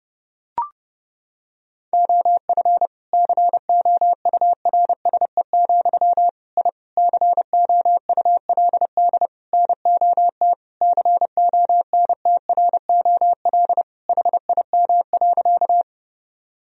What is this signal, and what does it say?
Morse code, heard as a keyed tone, about 30 wpm, 700 Hz. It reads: OF COURSE, I COULD NOT CONTROL HIM.
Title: Sentences from Top 1000 Words - Review Part 3 - 30wpm